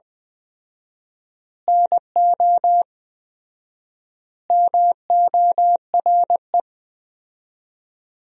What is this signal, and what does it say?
Morse code, heard as a keyed tone, about 20 wpm, 700 Hz. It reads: NO MORE
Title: 4x Word-Spacing - Sentences from Top 200 Words - 20wpm